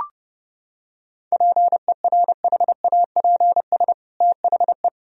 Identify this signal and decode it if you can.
Morse code, heard as a keyed tone, about 32 wpm, 700 Hz. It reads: PERHAPS THE
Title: Sentences from Top 1000 Words - Review Part 4 - 30wpm